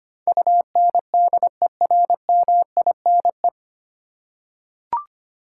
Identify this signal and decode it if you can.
Morse code, heard as a keyed tone, about 25 wpm, 700 Hz. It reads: UNDERMINE
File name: Nine Letter Words - 25wpm